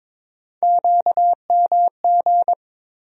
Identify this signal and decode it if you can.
Morse code, heard as a keyed tone, about 22 wpm, 700 Hz. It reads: QMG
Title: USA Call Sign Postfixes - 3 Letters 22wpm